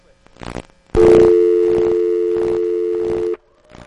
Dial tone (North America), 350 + 440 Hz.
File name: 2005 Defcon V63-Strom Carlsom, Black Ratchet-Asterisk